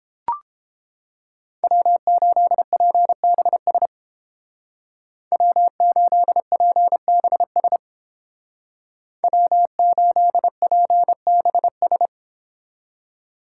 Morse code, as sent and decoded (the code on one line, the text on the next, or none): .-- ---.. .--. -... ... / .-- ---.. .--. -... ... / .-- ---.. .--. -... ...
W8PBS W8PBS W8PBS